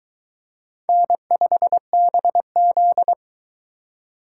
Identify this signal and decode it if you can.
Morse code, heard as a keyed tone, about 22 wpm, 700 Hz. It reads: N5BZ